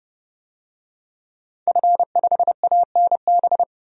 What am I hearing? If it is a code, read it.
Morse code, heard as a keyed tone, about 30 wpm, 700 Hz. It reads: F5ANB
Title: DXCC-100 Call Signs - 5 Characters - 30wpm